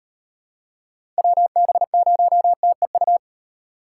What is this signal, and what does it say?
Morse code, heard as a keyed tone, about 35 wpm, 700 Hz. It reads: WB0TEU